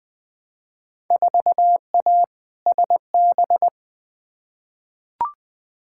Morse code, heard as a keyed tone, about 21 wpm, 700 Hz. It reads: 4A SB
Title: Field Day Exchanges Encore - 20wpm